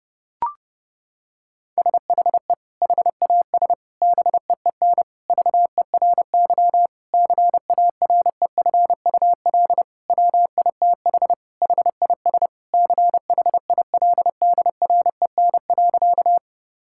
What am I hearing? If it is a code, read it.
Morse code, heard as a keyed tone, about 30 wpm, 700 Hz. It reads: SHE HAS BEEN VERY CAREFUL WITH HIS CHILDREN.
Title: Sentences from Top 1600 Words 30wpm